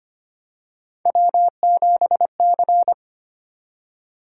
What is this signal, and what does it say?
Morse code, heard as a keyed tone, about 25 wpm, 700 Hz. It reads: W7C
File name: Three Letter-Number Combinations - 25wpm